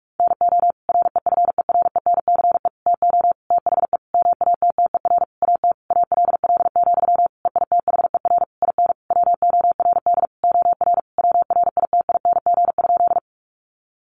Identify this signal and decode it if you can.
Morse code, heard as a keyed tone, about 44 wpm, 700 Hz. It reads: NO REFERENCE TO THE MATTER AT ALL, EITHER IN WORD OR WRITING?
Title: Sentences from Top 1200 Words 45wpm